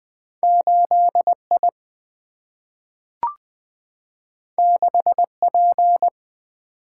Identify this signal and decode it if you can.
Morse code, heard as a keyed tone, about 20 wpm, 700 Hz. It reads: 8I 6P
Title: Single-Letter Single-Number 20wpm